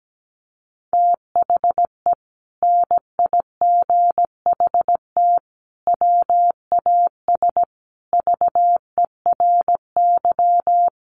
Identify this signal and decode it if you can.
Morse code, heard as a keyed tone, about 17 wpm, 700 Hz. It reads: THE NIGHT WAS VERY
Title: Sentences from Top 500 Words 17wpm